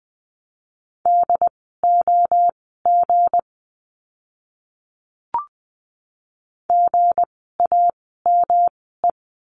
Morse code, heard as a keyed tone, about 19 wpm, spaced as for 14 wpm, 700 Hz. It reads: DOG GAME
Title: 160 Lesson 26f - Letter g - Words 20wpm